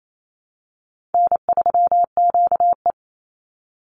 Morse code, heard as a keyed tone, about 28 wpm, 700 Hz. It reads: N3QE
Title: Easy Call Signs Repeated 3x - 28wpm